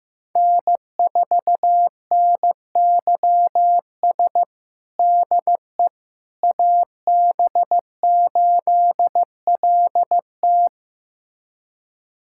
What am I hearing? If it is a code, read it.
Morse code, heard as a keyed tone, about 16 wpm, 700 Hz. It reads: N4NYS DE AB8LT
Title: USA Callsign DE Callsign - 5 Characters - 15wpm